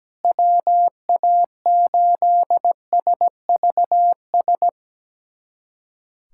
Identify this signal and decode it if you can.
Morse code, heard as a keyed tone, about 17 wpm, 700 Hz. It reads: WA8SVS
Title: USA Callsign DE Callsign 17wpm